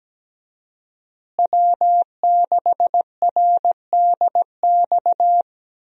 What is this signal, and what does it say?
Morse code, heard as a keyed tone, about 17 wpm, 700 Hz. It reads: W6RDX
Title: USA Call Signs - 5 Characters - Repeated 2x - 17wpm